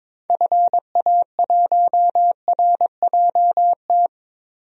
Morse code, heard as a keyed tone, about 22 wpm, 700 Hz. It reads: FA1RJT
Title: DXCC-100 Call Signs - 6 Characters - 22wpm